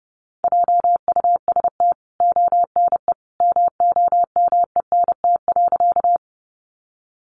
Morse code, heard as a keyed tone, about 30 wpm, 700 Hz. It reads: JUST ONE MOMENT.
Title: Sentences from Top 1000 Words 30wpm